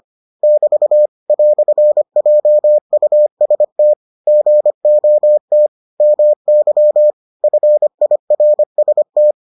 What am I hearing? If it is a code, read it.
Morse code, heard as a keyed tone, about 25 wpm, 600 Hz. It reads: <BT> "JUST GOT MY FIRST